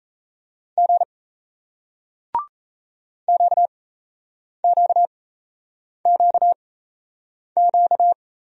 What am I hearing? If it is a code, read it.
Morse code, heard as a keyed tone, about 38 wpm, spaced as for 8 wpm, 700 Hz. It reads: G QQQQ